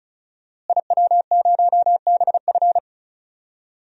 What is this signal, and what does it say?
Morse code, heard as a keyed tone, about 36 wpm, 700 Hz. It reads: IW0BF
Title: DXCC-100 Call Signs - 5 Characters - Repeated 2x - 35wpm